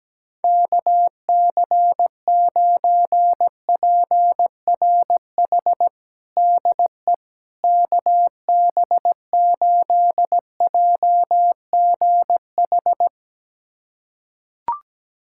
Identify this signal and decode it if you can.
Morse code, heard as a keyed tone, about 17 wpm, 700 Hz. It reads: KC9PRH DE KB8JGH
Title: USA Callsign DE Callsign - 6 Characters - 17wpm